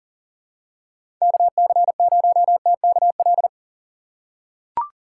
Morse code, heard as a keyed tone, about 40 wpm, 700 Hz. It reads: KC0TKL
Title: USA Call Signs - 6 Characters - 4x Character Spacing - 40wpm